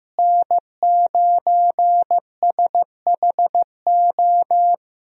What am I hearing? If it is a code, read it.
Morse code, heard as a keyed tone, about 15 wpm, 700 Hz. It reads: N9SHO
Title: USA Call Signs - 5 Characters - Repeated 2x - 15wpm